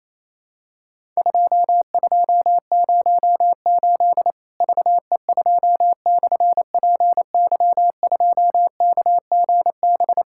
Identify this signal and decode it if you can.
Morse code, heard as a keyed tone, about 28 wpm, 700 Hz. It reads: 2208 VE2/PY2KGB
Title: Worked All Europe DX Contest - QTC Exchange - 28wpm